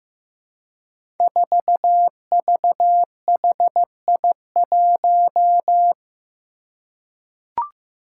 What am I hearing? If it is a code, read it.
Morse code, heard as a keyed tone, about 15 wpm, 700 Hz. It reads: 4VHI1
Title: Five Challenging Letter-Number Combinations 15wpm